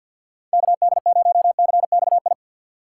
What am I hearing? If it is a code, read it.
Morse code, heard as a keyed tone, about 48 wpm, 700 Hz. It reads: KD0CXI
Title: USA Call Signs - 4x Character Spacing - 50wpm